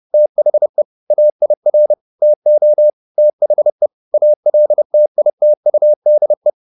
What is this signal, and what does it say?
Morse code, heard as a keyed tone, about 30 wpm, 600 Hz. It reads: THE AIR TO THE ALTITUDE